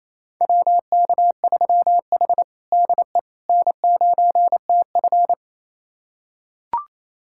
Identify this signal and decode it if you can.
Morse code, heard as a keyed tone, about 29 wpm, 690 Hz. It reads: WK3H DE N9TF
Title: USA Callsign DE Callsign - 4 Characters - 28wpm